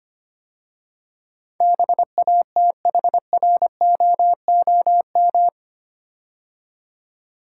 Morse code, heard as a keyed tone, about 25 wpm, 700 Hz. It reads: BATHROOM
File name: 8 Letter Words - First 3 Letters Given - 25wpm